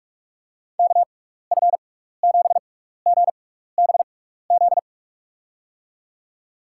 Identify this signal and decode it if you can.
Morse code, heard as a keyed tone, about 44 wpm, 700 Hz. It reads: K F 7 G B Z
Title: USA Call Signs - 6 Characters - 6x Character Spacing - 45wpm